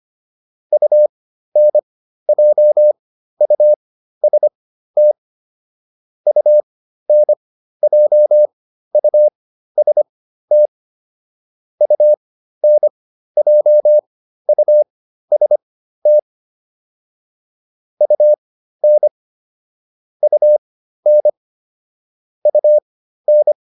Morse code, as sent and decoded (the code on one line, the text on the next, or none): ..- -. .--- ..- ... - / ..- -. .--- ..- ... - / ..- -. .--- ..- ... - / ..- -. / ..- -. / ..- -.
UNJUST UNJUST UNJUST UN UN UN